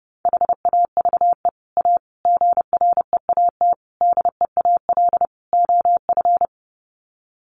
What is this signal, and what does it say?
Morse code, heard as a keyed tone, about 30 wpm, 700 Hz. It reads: HAVE A GREAT DEAL OF
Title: Sets of 5 Words Encore2 30wpm